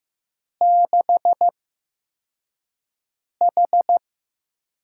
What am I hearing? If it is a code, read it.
Morse code, heard as a keyed tone, about 15 wpm, 700 Hz. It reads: 6 H
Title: USA Call Signs - 8x Character Spacing - 15wpm